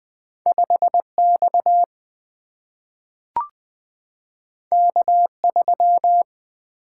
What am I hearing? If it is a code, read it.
Morse code, heard as a keyed tone, about 20 wpm, 700 Hz. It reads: 5X K3
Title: Single-Letter Single-Number 20wpm